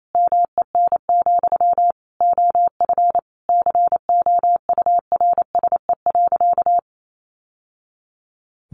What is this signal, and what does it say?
Morse code, heard as a keyed tone, about 28 wpm, 700 Hz. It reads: MEN, OF COURSE.
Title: Sentences from Top 1100 Words 28wpm